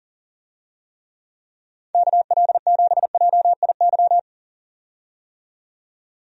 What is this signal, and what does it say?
Morse code, heard as a keyed tone, about 40 wpm, 700 Hz. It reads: KL7JIY
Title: USA Call Signs - 6 Characters - 40wpm